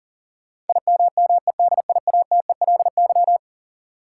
Morse code, heard as a keyed tone, about 40 wpm, 700 Hz. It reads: IMMEDIATELY
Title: Sentences from Top 1100 Words - New Word Drill Alt - 40wpm